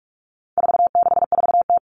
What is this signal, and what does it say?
Morse code, heard as a keyed tone, about 44 wpm, 700 Hz. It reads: 464T
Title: All Cut Numbers - 4-Digit 45wpm